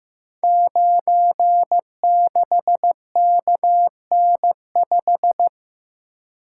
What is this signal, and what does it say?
Morse code, heard as a keyed tone, about 15 wpm, 700 Hz. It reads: 96KN5
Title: Code Groups - 5 Letter:Numbers - 15wpm